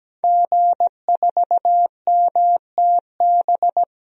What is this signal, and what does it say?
Morse code, heard as a keyed tone, about 17 wpm, 700 Hz. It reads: G4MTB
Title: DXCC-100 Call Signs - 5 Characters - 17wpm